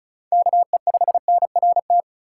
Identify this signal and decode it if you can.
Morse code, heard as a keyed tone, about 36 wpm, 700 Hz. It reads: KE5NRT
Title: USA Call Signs - 6 Characters - 4x Character Spacing - 35wpm